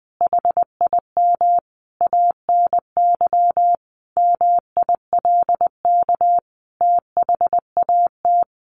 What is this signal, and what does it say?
Morse code, heard as a keyed tone, about 20 wpm, 700 Hz. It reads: HIM ANY MILK THAT
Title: Sentences from Top 1500 Words 20wpm